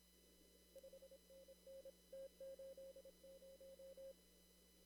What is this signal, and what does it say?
Morse code, heard as a keyed tone, about 27 wpm, 540 Hz. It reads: 5NN T80
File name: CW / kt5j wpx cw-op-k5pi